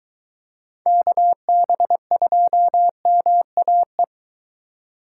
Morse code, heard as a keyed tone, about 23 wpm, 700 Hz. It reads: KB2MAE